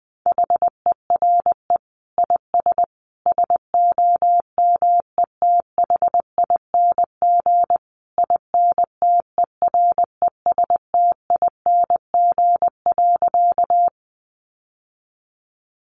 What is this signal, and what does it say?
Morse code, heard as a keyed tone, about 20 wpm, 700 Hz. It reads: HERE IS SOMETHING INTERESTING.
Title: Sentences from Top 1000 Words 20wpm